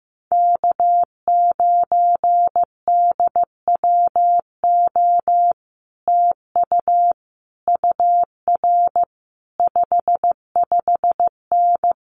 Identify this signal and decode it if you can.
Morse code, heard as a keyed tone, about 15 wpm, 700 Hz. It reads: K9DWO TU UR 55N
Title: Common QSO Phrases Encore 15wpm